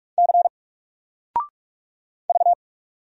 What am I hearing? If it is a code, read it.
Morse code, heard as a keyed tone, about 44 wpm, 700 Hz. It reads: C V
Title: Single Character (Letter:Number:Punctuation:Prosign) Alt 45wpm